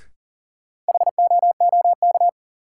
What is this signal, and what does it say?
Morse code, heard as a keyed tone, about 40 wpm, 700 Hz. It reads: HOOK
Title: Sentences from Top 1600 Words - New Word Drill Alt - 40wpm